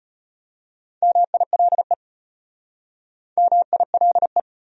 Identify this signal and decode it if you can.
Morse code, heard as a keyed tone, about 34 wpm, 700 Hz. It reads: MILE MILE